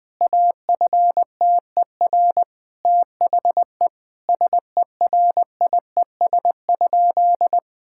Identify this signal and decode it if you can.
Morse code, heard as a keyed tone, about 20 wpm, 700 Hz. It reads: AFTER THE SERIES?
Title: Sentences from Top 1000 Words - Review Part 5 - 20wpm